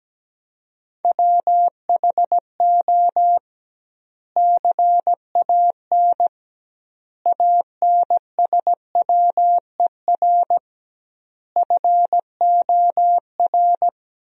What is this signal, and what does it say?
Morse code, heard as a keyed tone, about 17 wpm, 700 Hz. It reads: WHO CAN ANSWER FOR
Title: 2x Word-Spacing - Sentences from Top 300 Words - 17wpm